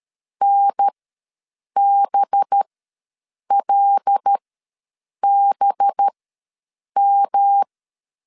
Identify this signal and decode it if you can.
Morse code, heard as a keyed tone, about 13 wpm, spaced as for 7 wpm, 800 Hz. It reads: NBLBM
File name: 5 LB ws 12wpm